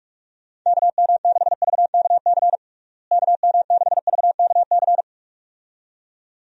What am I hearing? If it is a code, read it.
Morse code, heard as a keyed tone, about 41 wpm, 700 Hz. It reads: KM6VKC KM6VKC
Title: USA Call Signs - 6 Characters - Repeated 2x - 45wpm